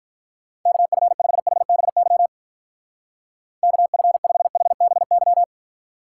Morse code, heard as a keyed tone, about 45 wpm, 700 Hz. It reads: KF5HBY KF5HBY